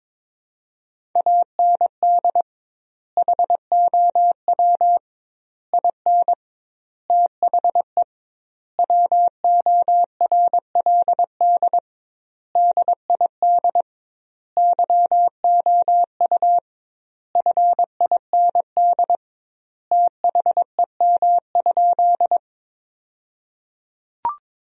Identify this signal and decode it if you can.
Morse code, heard as a keyed tone, about 22 wpm, 700 Hz. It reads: AND HOW IN THE WORLD DID YOU FIND THEM?
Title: 2x Word-Spacing - Sentences from Top 200 Words - 22wpm